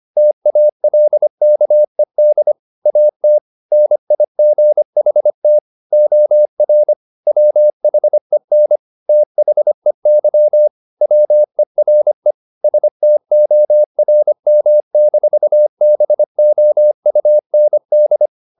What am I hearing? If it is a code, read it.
Morse code, heard as a keyed tone, about 25 wpm, 600 Hz. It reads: TALKED AT NIGHT OR WHEN THEY WERE STORM-BOUND
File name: the old man and the sea part 1 25wpm